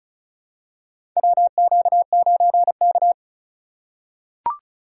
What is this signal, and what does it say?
Morse code, heard as a keyed tone, about 34 wpm, 700 Hz. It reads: WQ9K
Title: USA Call Signs - 4 Characters - 4x Character Spacing - 35wpm